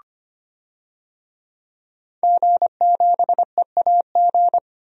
Morse code, heard as a keyed tone, about 25 wpm, 700 Hz. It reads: G7EAG
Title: DXCC-100 Call Signs - 5 Characters - 25wpm